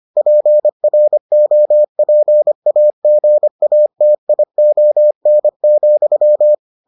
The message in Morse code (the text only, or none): PROPAGATION,